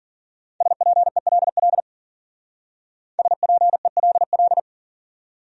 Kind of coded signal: Morse code

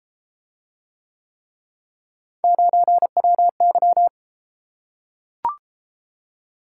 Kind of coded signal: Morse code

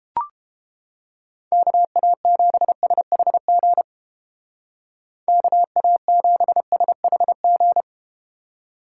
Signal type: Morse code

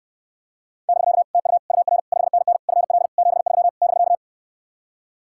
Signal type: Morse code

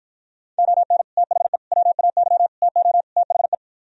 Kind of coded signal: Morse code

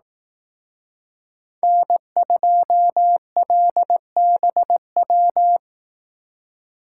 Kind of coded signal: Morse code

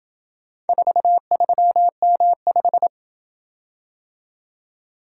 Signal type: Morse code

